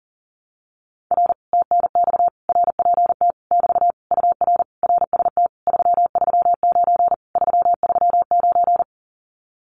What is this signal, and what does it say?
Morse code, heard as a keyed tone, about 38 wpm, 700 Hz. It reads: R TNX RPT <BT> UR RST 339 339
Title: Common QSO Phrases Encore 40wpm